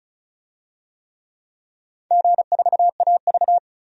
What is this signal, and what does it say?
Morse code, heard as a keyed tone, about 34 wpm, 700 Hz. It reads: G4AV
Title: DXCC-100 Call Signs - 4 Characters - 35wpm